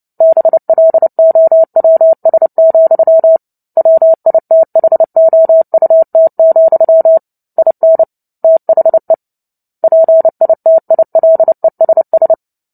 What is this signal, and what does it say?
Morse code, heard as a keyed tone, about 30 wpm, 650 Hz. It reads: BLOWS, WITHOUT, IN THE PITILESS
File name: Book Two / 03-Chapter 3 - 29 WPM